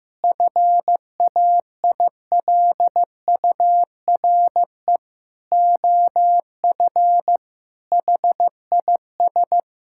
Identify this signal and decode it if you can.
Morse code, heard as a keyed tone, about 16 wpm, 700 Hz. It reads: FAILURE OF HIS
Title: Sentences from Top 1000 Words - Review Part 6 - 15wpm